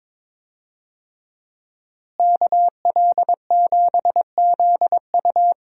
Morse code, heard as a keyed tone, about 22 wpm, 700 Hz. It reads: KL7ZU
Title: USA Call Signs - 5 Characters - Repeated 2x - 22wpm